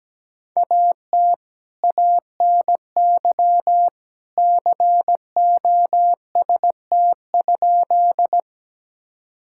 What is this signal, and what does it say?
Morse code, heard as a keyed tone, about 17 wpm, 700 Hz. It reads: AT ANY COST?
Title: Sentences from Top 1000 Words - Review Part 5 - 17wpm